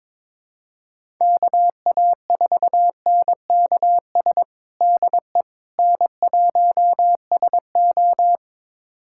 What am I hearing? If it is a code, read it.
Morse code, heard as a keyed tone, about 22 wpm, 700 Hz. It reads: KA4NKS DE N1SO
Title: USA Callsign DE Callsign 22wpm